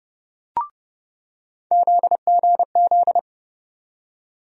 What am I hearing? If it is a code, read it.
Morse code, heard as a keyed tone, about 30 wpm, 700 Hz. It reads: ZGZ
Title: USA Call Sign Postfixes - 3 Letters 30wpm